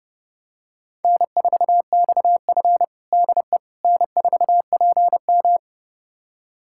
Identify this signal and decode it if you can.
Morse code, heard as a keyed tone, about 32 wpm, 700 Hz. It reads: N4XF DE N4PM
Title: USA Callsign DE Callsign - 4 Characters - 30wpm